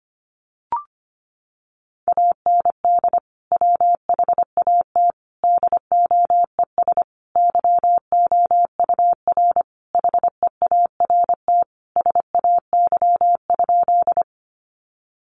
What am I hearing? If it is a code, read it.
Morse code, heard as a keyed tone, about 25 wpm, 700 Hz. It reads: AND WHAT DOES YOUR HEART SAY?
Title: Sentences from Top 1200 Words 25wpm